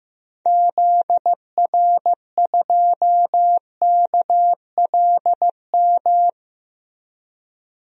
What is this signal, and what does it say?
Morse code, heard as a keyed tone, about 15 wpm, 700 Hz. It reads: ZR2KLM
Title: DXCC-100 Call Signs - 6 Characters - Repeated 2x - 15wpm